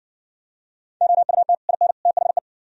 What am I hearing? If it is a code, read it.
Morse code, heard as a keyed tone, about 60 wpm, 700 Hz. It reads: OUT IN THE